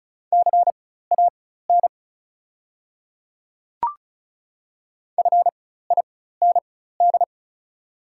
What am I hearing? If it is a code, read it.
Morse code, heard as a keyed tone, about 35 wpm, spaced as for 16 wpm, 690 Hz. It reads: CAN FIND
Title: 4x Character Spacing - Top 100 Words - 35wpm